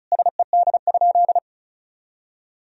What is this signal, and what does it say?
Morse code, heard as a keyed tone, about 34 wpm, 700 Hz. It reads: SED?
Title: Sentences from Top 100 Words and Abbreviations 35wpm